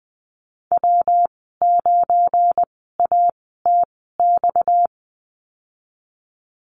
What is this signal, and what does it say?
Morse code, heard as a keyed tone, about 20 wpm, spaced as for 14 wpm, 700 Hz. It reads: W9ATX
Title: 238 Lesson 38g - Letter q - Call Signs 20wpm